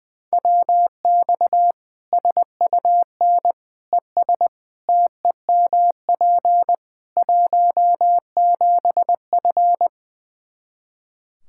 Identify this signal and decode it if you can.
Morse code, heard as a keyed tone, about 20 wpm, 700 Hz. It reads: WX SUN ES TEMP 17F
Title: Practice QSO - Exchange 2 - Info - 20wpm